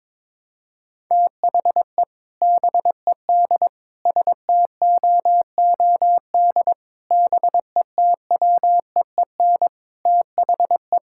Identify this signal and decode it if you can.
Morse code, heard as a keyed tone, about 22 wpm, 700 Hz. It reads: THE BED STOOD BETWEEN THE
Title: Sentences from Top 1600 Words 22wpm